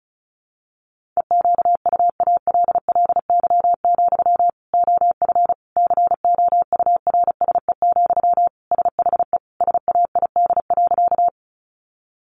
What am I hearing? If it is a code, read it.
Morse code, heard as a keyed tone, about 34 wpm, 700 Hz. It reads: EQUALLY, OF COURSE, SHE SAID.
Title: Sentences from Top 1400 Words 35wpm